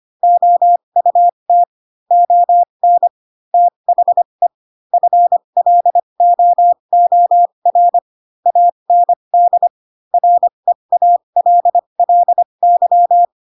Morse code, heard as a keyed tone, about 25 wpm, spaced as for 22 wpm, 700 Hz. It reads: OUT ON THE FLOOR AND REALLY
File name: Pinocchio / Pinocchio ch7 18 25 ik1hge